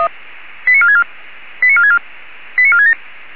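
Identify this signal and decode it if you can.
Five-tone selective call (ZVEI), heard as quick runs of five tones, tones of about 70 ms. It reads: 89464 89464 89467
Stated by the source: ZVEI five-tone selective call received by SDR.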